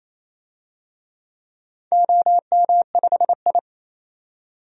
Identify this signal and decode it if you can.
Morse code, heard as a keyed tone, about 28 wpm, 700 Hz. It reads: OM5I
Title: DXCC-100 Call Signs - 4 Characters - 28wpm